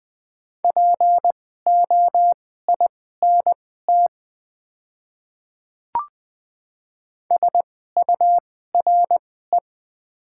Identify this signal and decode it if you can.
Morse code, heard as a keyed tone, about 20 wpm, spaced as for 14 wpm, 690 Hz. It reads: POINT SURE
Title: 2x Character Spacing - Top 300 Words - 20wpm